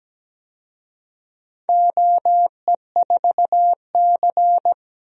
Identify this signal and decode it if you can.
Morse code, heard as a keyed tone, about 17 wpm, 700 Hz. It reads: OE4C
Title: DXCC-100 Call Signs - 4 Characters - 17wpm